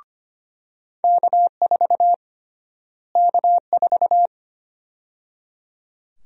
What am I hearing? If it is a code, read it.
Morse code, heard as a keyed tone, about 25 wpm, 700 Hz. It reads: K4 K4
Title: USA Call Sign Prefixes - Repeated 2x - 25wpm